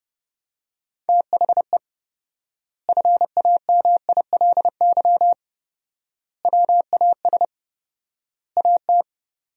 Morse code, heard as a keyed tone, about 30 wpm, 700 Hz. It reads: THE FAMILY WAS AT
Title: 4x Word-Spacing - Sentences from Top 400 Words - 30wpm